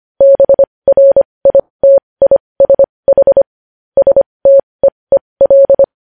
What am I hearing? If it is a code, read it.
Morse code, heard as a keyed tone, about 25 wpm, spaced as for 20 wpm, 550 Hz. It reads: BRITISH STEEL
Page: news headlines in morse code at 20 wpm